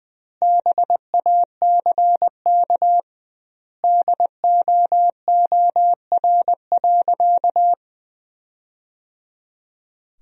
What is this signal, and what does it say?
Morse code, heard as a keyed tone, about 21 wpm, 700 Hz. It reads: BACK DOOR.
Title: 2x Word-Spacing - Sentences from Top 300 Words - 20wpm